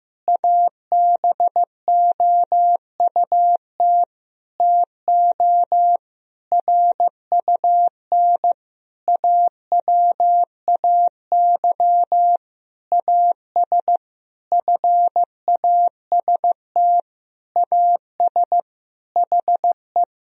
Morse code, heard as a keyed tone, about 15 wpm, 700 Hz. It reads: ABOUT TO RUN AWAY AS FAST AS HE
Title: Sentences from Top 1000 Words - Review Part 2 - 15wpm